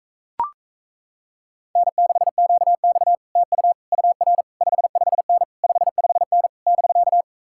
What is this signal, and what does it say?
Morse code, heard as a keyed tone, about 38 wpm, 700 Hz. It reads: N6QX TU UR 55N 55N <BK>